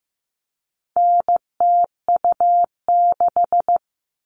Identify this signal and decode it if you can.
Morse code, heard as a keyed tone, about 15 wpm, 700 Hz. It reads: NTU6
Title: All Cut Numbers - 4-Digit 15wpm